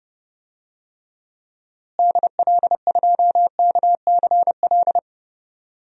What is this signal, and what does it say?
Morse code, heard as a keyed tone, about 32 wpm, 700 Hz. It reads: DL2KCL
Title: DXCC-100 Call Signs - 6 Characters - Repeated 3x - 30wpm